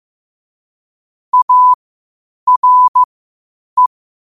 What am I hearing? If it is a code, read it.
Morse code, heard as a keyed tone, about 15 wpm, spaced as for 8 wpm, 1000 Hz. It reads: ARE
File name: qotdfiles / 8wpm